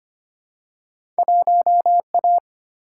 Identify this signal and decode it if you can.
Morse code, heard as a keyed tone, about 25 wpm, 700 Hz. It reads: 1A
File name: Two Letter-Number Combinations - 25wpm